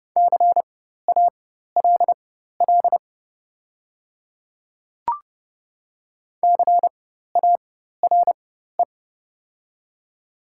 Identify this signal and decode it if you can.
Morse code, heard as a keyed tone, about 30 wpm, spaced as for 14 wpm, 700 Hz. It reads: CALL CARE
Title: 4x Character Spacing - Top 500 Words - 30wpm